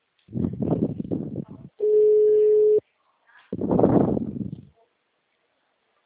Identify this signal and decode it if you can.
Ringback tone (Europe, 425 Hz).